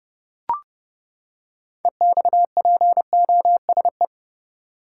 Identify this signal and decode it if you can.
Morse code, heard as a keyed tone, about 30 wpm, 700 Hz. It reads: EXPOSE
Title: Six Letter Words - 30wpm